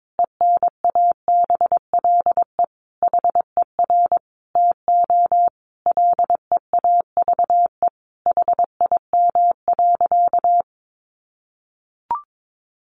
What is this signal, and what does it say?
Morse code, heard as a keyed tone, about 22 wpm, 700 Hz. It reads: ENABLE HER TO LEAVE HIM.
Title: Sentences from Top 1500 Words 22wpm